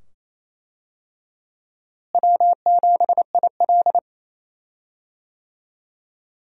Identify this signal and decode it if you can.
Morse code, heard as a keyed tone, about 28 wpm, 700 Hz. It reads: W7IL
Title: USA Call Signs - 4 Characters - Repeated 2x - 28wpm